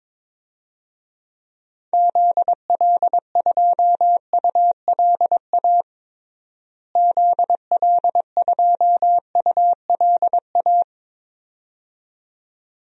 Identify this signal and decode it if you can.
Morse code, heard as a keyed tone, about 22 wpm, 700 Hz. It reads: ZL2ULA ZL2ULA
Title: DXCC-100 Call Signs - 6 Characters - Repeated 2x - 22wpm